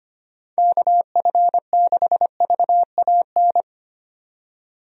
Morse code, heard as a keyed tone, about 25 wpm, 700 Hz. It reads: KF6VAN
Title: USA Call Signs - 6 Characters - 25wpm